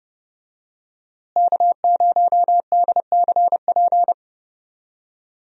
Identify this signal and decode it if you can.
Morse code, heard as a keyed tone, about 30 wpm, 700 Hz. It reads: K0DCP
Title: USA Call Signs - 5 Characters - Repeated 2x - 30wpm